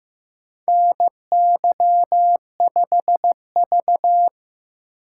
Morse code, heard as a keyed tone, about 15 wpm, 700 Hz. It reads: NY5V
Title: USA Call Signs - 4x Character Spacing - 15wpm